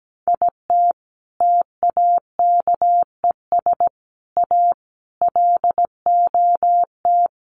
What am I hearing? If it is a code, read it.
Morse code, heard as a keyed tone, about 17 wpm, 700 Hz. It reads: IT TAKES A LOT
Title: Sets of 5 Words Encore 17wpm